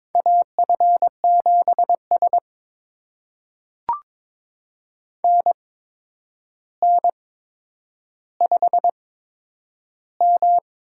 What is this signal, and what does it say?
Morse code, heard as a keyed tone, about 22 wpm, spaced as for 9 wpm, 700 Hz. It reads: AF7S NN5M
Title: USA Call Signs - 4 Characters - 8x Character Spacing - 22wpm